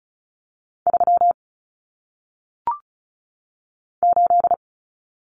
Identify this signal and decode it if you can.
Morse code, heard as a keyed tone, about 36 wpm, 700 Hz. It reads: 3 8
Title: Common Cut Numbers - 1-Digit 35wpm